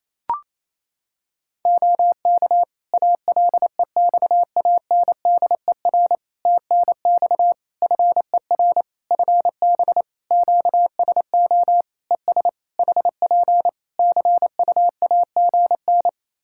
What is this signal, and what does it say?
Morse code, heard as a keyed tone, about 28 wpm, 700 Hz. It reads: OK ALEXANDER TNX FER FB QSO ES HP CUAGN
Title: K1USN SST QSO Practice 28wpm